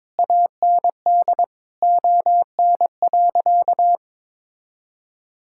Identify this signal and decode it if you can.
Morse code, heard as a keyed tone, about 22 wpm, 700 Hz. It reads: AND ON.
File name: practice / 22wpm / Sentences with 8-Word Phrases - 22wpm